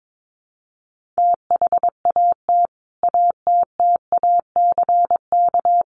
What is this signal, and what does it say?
Morse code, heard as a keyed tone, about 22 wpm, 700 Hz. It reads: THAT ATTACK
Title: Sentences from Top 1500 Words 22wpm